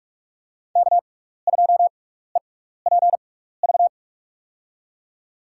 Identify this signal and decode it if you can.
Morse code, heard as a keyed tone, about 45 wpm, spaced as for 15 wpm, 690 Hz. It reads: K2EPV
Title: USA Call Signs - 5 Characters - 6x Character Spacing - 45wpm